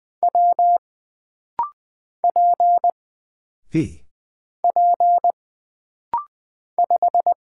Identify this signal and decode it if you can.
Morse code, heard as a keyed tone, about 20 wpm, 700 Hz. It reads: W P P 5
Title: Single Character (Letter:Number:Punctuation:Prosign) - Rapid-Fire - 20wpm